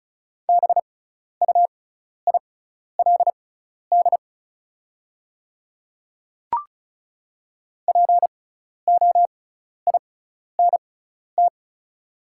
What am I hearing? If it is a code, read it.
Morse code, heard as a keyed tone, about 36 wpm, spaced as for 12 wpm, 690 Hz. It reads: BUILD POINT
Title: 6x Character Spacing - Top 200 Words - 35wpm